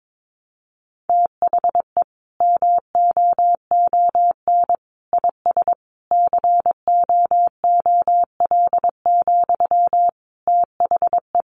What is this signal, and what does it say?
Morse code, heard as a keyed tone, about 22 wpm, 700 Hz. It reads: THE MOON IS COOL, THE
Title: Sentences from Top 1200 Words 22wpm